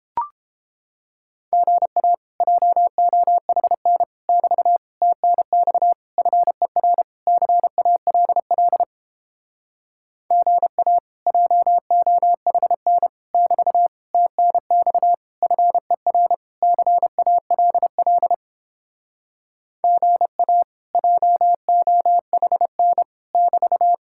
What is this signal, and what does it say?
Morse code, heard as a keyed tone, about 29 wpm, 700 Hz. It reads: GA JOHN <BT> TNX FER CALL GA JOHN <BT> TNX FER CALL GA JOHN <BT>